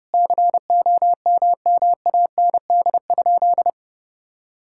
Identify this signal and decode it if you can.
Morse code, heard as a keyed tone, about 30 wpm, 700 Hz. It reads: COMMAND?
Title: Sentences from Top 1600 Words 30wpm